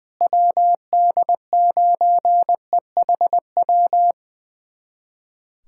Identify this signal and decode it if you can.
Morse code, heard as a keyed tone, about 21 wpm, 700 Hz. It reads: WD9EHW
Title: USA Callsign DE Callsign 20wpm